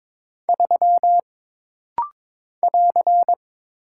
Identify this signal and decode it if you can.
Morse code, heard as a keyed tone, about 22 wpm, 700 Hz. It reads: 3 <AR>
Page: This rapid-fire timing is geared to push your ICR (Instant Character Recognition) skills.